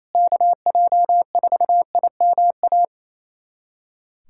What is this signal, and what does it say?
Morse code, heard as a keyed tone, about 28 wpm, 700 Hz. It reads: KJ4IMA
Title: USA Callsign DE Callsign - 6 Characters - 28wpm